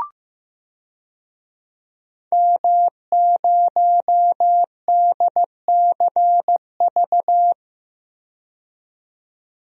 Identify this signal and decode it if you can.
Morse code, heard as a keyed tone, about 16 wpm, 700 Hz. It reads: M0DCV
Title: DXCC-100 Call Signs - 5 Characters - 15wpm